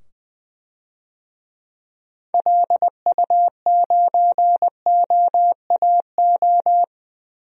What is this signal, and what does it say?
Morse code, heard as a keyed tone, about 20 wpm, 700 Hz. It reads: LU9OAO
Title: DXCC-100 Call Signs - 6 Characters - 20wpm